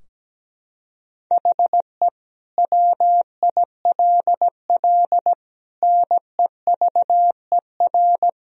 Morse code, heard as a keyed tone, about 17 wpm, 700 Hz. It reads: HE WILL NEVER
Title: Sentences from Top 1000 Words - Review Part 3 - 17wpm